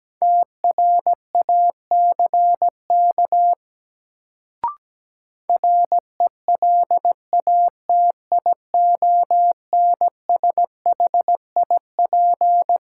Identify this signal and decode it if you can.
Morse code, heard as a keyed tone, about 17 wpm, 700 Hz. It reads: TRACK RELATIONSHIP
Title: Sentences from Top 1100 Words - New Word Drill - 17wpm